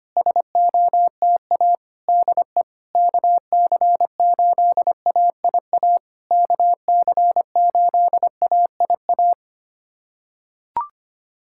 Morse code, heard as a keyed tone, about 25 wpm, 700 Hz. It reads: SOTA DE KC8AIA KC8AIA
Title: SOTA QSO Practice 25wpm